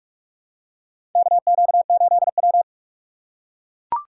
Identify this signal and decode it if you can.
Morse code, heard as a keyed tone, about 44 wpm, 700 Hz. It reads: KQ8W
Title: USA Call Signs - 4 Characters - 6x Character Spacing - 45wpm